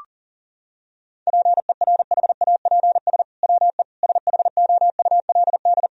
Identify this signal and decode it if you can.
Morse code, heard as a keyed tone, about 40 wpm, 700 Hz. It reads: PERHAPS WE SHOULD
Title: Sentences from Top 1000 Words - Review Part 3 - 40wpm